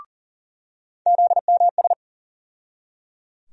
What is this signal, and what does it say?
Morse code, heard as a keyed tone, about 37 wpm, 700 Hz. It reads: ZMS
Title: USA Call Sign Postfixes - 3 Letters 40wpm